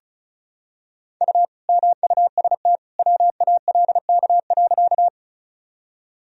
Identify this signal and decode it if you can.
Morse code, heard as a keyed tone, about 36 wpm, 700 Hz. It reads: U MUST WALK.
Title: Sentences from Top 300 Words and Abbreviations 35wpm